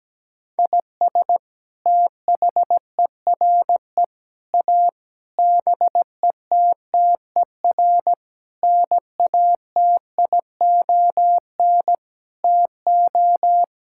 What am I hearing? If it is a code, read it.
Morse code, heard as a keyed tone, about 17 wpm, 700 Hz. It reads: IS THERE A BETTER NATION TO
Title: Sentences from Top 1000 Words - Review Part 6 - 17wpm